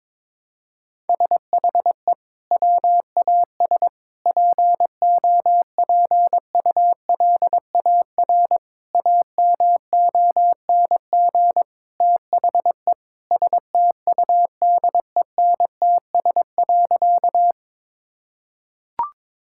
Morse code, heard as a keyed tone, about 22 wpm, 700 Hz. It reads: SHE WAS POPULAR AMONG THE STUDENTS.
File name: practice / 22wpm / Sentences with 2-Word Phrases - 22wpm